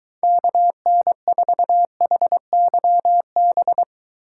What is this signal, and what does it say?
Morse code, heard as a keyed tone, about 23 wpm, 700 Hz. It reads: KN4HYB